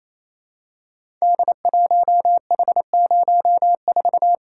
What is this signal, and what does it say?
Morse code, heard as a keyed tone, about 28 wpm, 700 Hz. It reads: D1H04
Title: Code Groups - 5 Letter:Numbers - 28wpm